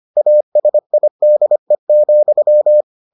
Morse code, heard as a keyed tone, about 25 wpm, 600 Hz. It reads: ASIDE,